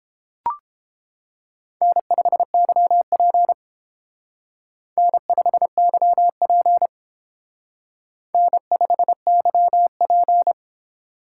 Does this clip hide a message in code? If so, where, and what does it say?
Yes: Morse code from 1.8 to 10.5 s, heard as a keyed tone, about 29 wpm, 700 Hz. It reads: N5YP N5YP N5YP